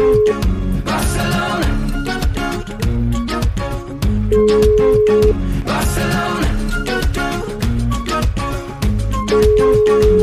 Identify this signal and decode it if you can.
Ringback tone (Europe, 425 Hz), with noise.